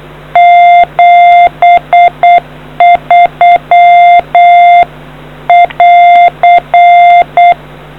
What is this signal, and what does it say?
Morse code, heard as a keyed tone, about 8 wpm, 700 Hz. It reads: 73 <AR>